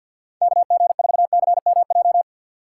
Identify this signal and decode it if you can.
Morse code, heard as a keyed tone, about 50 wpm, 700 Hz. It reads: KG4CMJ
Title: USA Call Signs - 6 Characters - 6x Character Spacing - 50wpm